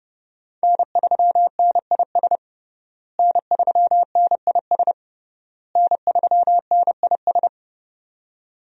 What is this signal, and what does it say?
Morse code, heard as a keyed tone, about 28 wpm, 700 Hz. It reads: N3NIS N3NIS N3NIS
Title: USA Call Signs - 5 Characters - Repeated 3x - 30wpm